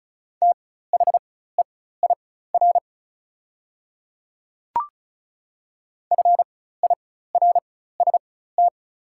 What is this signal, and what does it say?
Morse code, heard as a keyed tone, about 36 wpm, spaced as for 16 wpm, 700 Hz. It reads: THEIR FIRST
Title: 4x Character Spacing - Top 100 Words - 35wpm